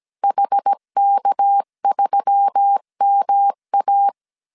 Morse code, heard as a keyed tone, about 17 wpm, 800 Hz. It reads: HK3MA
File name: CW / DET / 1 6 1500C S C 16wpm 2733